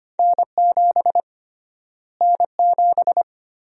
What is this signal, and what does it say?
Morse code, heard as a keyed tone, about 25 wpm, 700 Hz. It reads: N7 N7
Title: USA Call Sign Prefixes - Repeated 2x - 25wpm